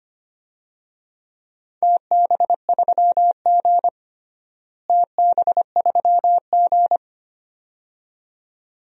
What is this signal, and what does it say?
Morse code, heard as a keyed tone, about 25 wpm, 700 Hz. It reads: TB3G TB3G
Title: DXCC-100 Call Signs - 4 Characters - Repeated 2x - 25wpm